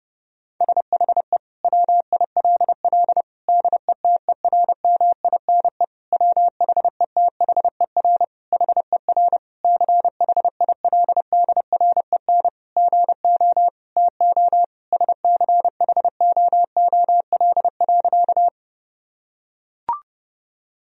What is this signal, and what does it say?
Morse code, heard as a keyed tone, about 30 wpm, 700 Hz. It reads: SHE WILL DETERMINE WHETHER HER CHILDREN GO TO SCHOOL.
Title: Sentences from Top 1000 Words - Review Part 4 - 30wpm